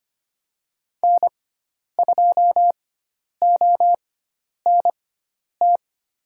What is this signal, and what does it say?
Morse code, heard as a keyed tone, about 25 wpm, spaced as for 10 wpm, 700 Hz. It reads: N2ONT
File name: Lesson 16g - Letter c - Call Signs 25wpm